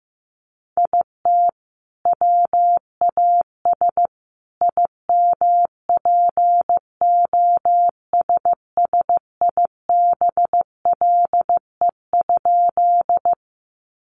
Morse code, heard as a keyed tone, about 15 wpm, 700 Hz. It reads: IT WAS IMPOSSIBLE?
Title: Sentences from Top 1100 Words 15wpm